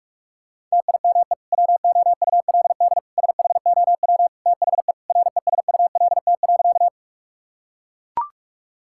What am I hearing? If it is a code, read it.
Morse code, heard as a keyed tone, about 41 wpm, 700 Hz. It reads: TIME WOULD SHOW THE RESULT.
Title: Sentences from Top 1000 Words - Review Part 5 - 45wpm